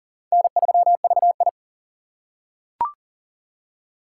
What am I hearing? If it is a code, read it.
Morse code, heard as a keyed tone, about 43 wpm, 690 Hz. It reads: N3VI
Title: USA Call Signs - 4 Characters - 2x Character Spacing - 40wpm